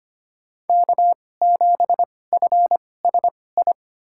Morse code, heard as a keyed tone, about 25 wpm, spaced as for 18 wpm, 700 Hz. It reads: K7FSI
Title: USA Call Signs - 5 Characters - 2x Character Spacing - 25wpm